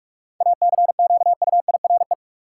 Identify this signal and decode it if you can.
Morse code, heard as a keyed tone, about 46 wpm, 700 Hz. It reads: ACQUIRE
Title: Seven Letter Words - 45wpm